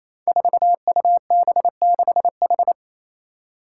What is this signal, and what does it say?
Morse code, heard as a keyed tone, about 28 wpm, 700 Hz. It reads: 4UB6H
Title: Five Challenging Letter-Number Combinations 28wpm